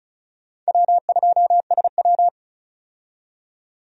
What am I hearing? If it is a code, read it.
Morse code, heard as a keyed tone, about 36 wpm, 700 Hz. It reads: W2SW
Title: USA Call Signs - 4 Characters - 35wpm